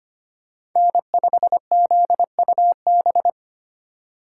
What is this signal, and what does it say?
Morse code, heard as a keyed tone, about 25 wpm, 700 Hz. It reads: N5ZUB
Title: USA Call Signs - 2x Character Spacing - 25wpm